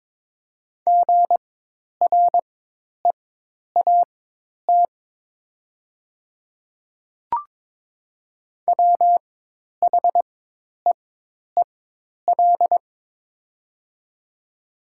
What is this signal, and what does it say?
Morse code, heard as a keyed tone, about 22 wpm, spaced as for 10 wpm, 700 Hz. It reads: GREAT WHEEL
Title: 4x Character Spacing - Top 500 Words - 22wpm